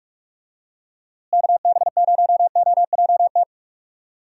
Morse code, heard as a keyed tone, about 44 wpm, 700 Hz. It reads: KB0OJT
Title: USA Call Signs - 6 Characters - Repeated 2x - 45wpm